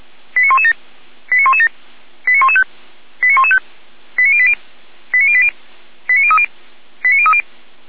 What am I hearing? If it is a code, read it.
Five-tone selective call (ZVEI), heard as quick runs of five tones, tones of about 80 ms. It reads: 89117 89117 89115 89115 89080 89080 89030 89030